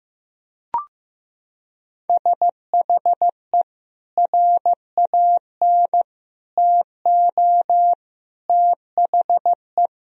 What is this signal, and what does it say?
Morse code, heard as a keyed tone, about 16 wpm, 700 Hz. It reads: SHE RAN TO THE
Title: Sentences from Top 1000 Words - Review Part 3 - 15wpm